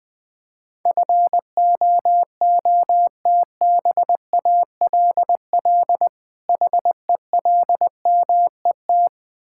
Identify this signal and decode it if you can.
Morse code, heard as a keyed tone, about 21 wpm, 700 Hz. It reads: FOOTBALL HELMET
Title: Difficult Words with Hints - 20wpm